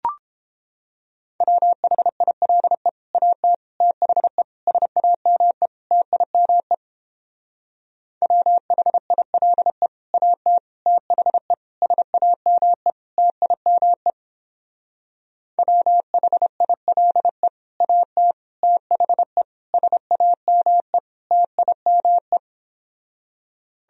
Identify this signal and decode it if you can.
Morse code, heard as a keyed tone, about 29 wpm, 700 Hz. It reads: WHILE AT THE SAME TIME WHILE AT THE SAME TIME WHILE AT THE SAME TIME